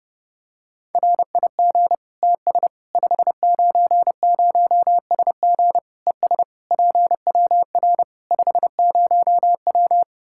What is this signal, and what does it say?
Morse code, heard as a keyed tone, about 32 wpm, 700 Hz. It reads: RIG TS 590SG ES PWR 50W
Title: K1USN SST QSO Practice 30wpm